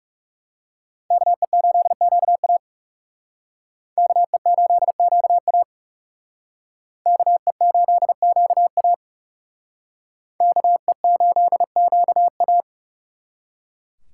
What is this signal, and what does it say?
Morse code, heard as a keyed tone, about 39 wpm, 700 Hz. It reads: KE8QA KE8QA KE8QA KE8QA